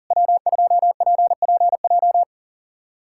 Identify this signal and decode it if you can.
Morse code, heard as a keyed tone, about 44 wpm, 700 Hz. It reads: W2PPJ
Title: SOTA QSO Practice 40wpm